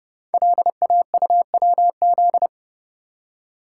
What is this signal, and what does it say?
Morse code, heard as a keyed tone, about 30 wpm, 700 Hz. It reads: LAUWZ
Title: Code Groups - 5 Letter:Numbers - 30wpm